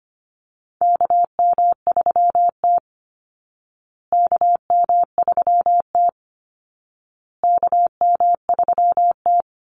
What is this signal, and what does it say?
Morse code, heard as a keyed tone, about 25 wpm, 700 Hz. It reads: KM3T KM3T KM3T
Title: Call Signs Repeated 2x - 25wpm